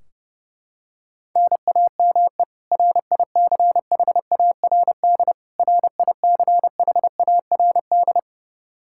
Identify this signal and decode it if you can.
Morse code, heard as a keyed tone, about 32 wpm, 700 Hz. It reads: NAME RICHARD RICHARD
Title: Practice QSO - Exchange 1 - Beginning - 30wpm